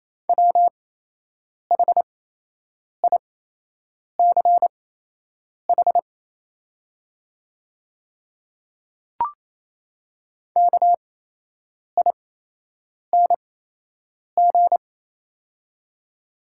Morse code, heard as a keyed tone, about 28 wpm, spaced as for 8 wpm, 700 Hz. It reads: WHICH KING
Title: 8x Character Spacing - Top 500 Words - 28wpm